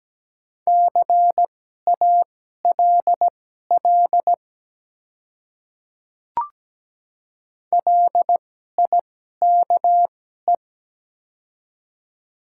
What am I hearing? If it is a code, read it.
Morse code, heard as a keyed tone, about 17 wpm, spaced as for 12 wpm, 700 Hz. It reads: CALL LIKE
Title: 2x Character Spacing - Top 100 Words - 17wpm